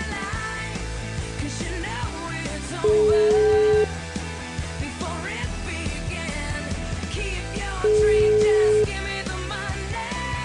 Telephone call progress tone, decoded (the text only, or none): ringback tone